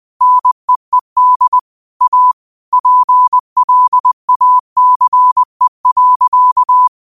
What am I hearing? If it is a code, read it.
Morse code, heard as a keyed tone, about 20 wpm, 1000 Hz. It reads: NEED A PLACE.